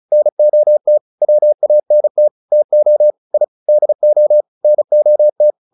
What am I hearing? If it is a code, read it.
Morse code, heard as a keyed tone, about 34 wpm, 600 Hz. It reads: NOT WANT TO I DO NOT